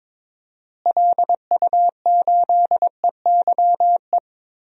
Morse code, heard as a keyed tone, about 22 wpm, 700 Hz. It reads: LU8EYE
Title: DXCC-100 Call Signs - 6 Characters - Repeated 2x - 22wpm